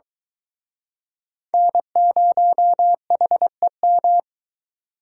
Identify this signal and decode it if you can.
Morse code, heard as a keyed tone, about 24 wpm, 700 Hz. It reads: N0HEM